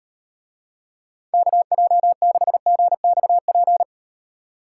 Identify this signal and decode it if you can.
Morse code, heard as a keyed tone, about 38 wpm, 700 Hz. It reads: KJ6GXP